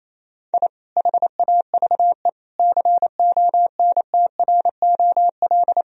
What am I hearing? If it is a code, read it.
Morse code, heard as a keyed tone, about 28 wpm, 700 Hz. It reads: I HAVE CONTROL
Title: Sentences from Top 1000 Words - Review Part 3 - 28wpm